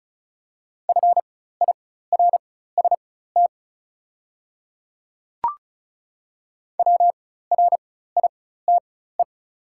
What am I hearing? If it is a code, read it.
Morse code, heard as a keyed tone, about 36 wpm, spaced as for 16 wpm, 700 Hz. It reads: FIRST WRITE
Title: 4x Character Spacing - Top 100 Words - 35wpm